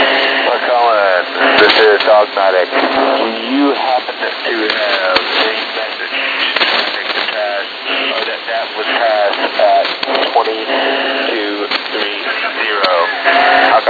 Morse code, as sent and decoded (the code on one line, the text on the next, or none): none